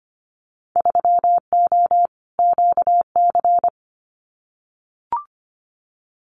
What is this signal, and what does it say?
Morse code, heard as a keyed tone, about 25 wpm, 690 Hz. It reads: 3O QC
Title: Winter Field Day Exchanges Encore 25wpm